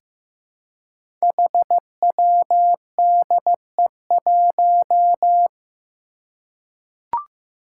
Morse code, heard as a keyed tone, about 15 wpm, 700 Hz. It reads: HWDE1
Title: Five Challenging Letter-Number Combinations 15wpm